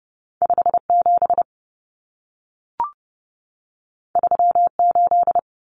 Morse code, heard as a keyed tone, about 30 wpm, 700 Hz. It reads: 57 38
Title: Common Cut Numbers - 2-Digit 30wpm